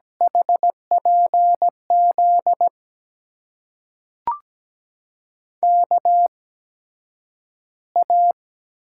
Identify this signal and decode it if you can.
Morse code, heard as a keyed tone, about 17 wpm, spaced as for 9 wpm, 700 Hz. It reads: HPZ KA
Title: USA Call Signs - 6 Characters - 8x Character Spacing - 17wpm